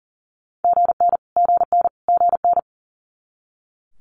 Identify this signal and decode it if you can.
Morse code, heard as a keyed tone, about 37 wpm, 690 Hz. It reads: GN GN GN
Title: Multi-letter Phonemes Repeated 3x 40wpm